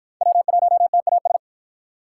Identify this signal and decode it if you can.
Morse code, heard as a keyed tone, about 49 wpm, 700 Hz. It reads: W1TAS